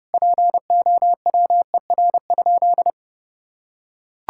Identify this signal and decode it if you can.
Morse code, heard as a keyed tone, about 30 wpm, 700 Hz. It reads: POWER?
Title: Sentences from Top 1000 Words - Review Part 5 - 30wpm